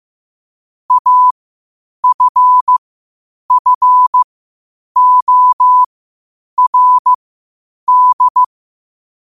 Morse code, heard as a keyed tone, about 15 wpm, spaced as for 8 wpm, 1000 Hz. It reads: AFFORD
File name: qotdfiles / 8wpm